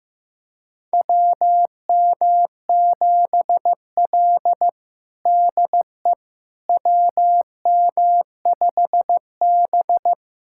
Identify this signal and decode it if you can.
Morse code, heard as a keyed tone, about 15 wpm, 700 Hz. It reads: WM7L DE WM5B
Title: USA Callsign DE Callsign - 4 Characters - 15wpm